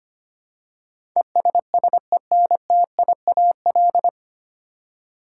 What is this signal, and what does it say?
Morse code, heard as a keyed tone, about 25 wpm, 700 Hz. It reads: ESSENTIAL
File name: Nine Letter Words - 25wpm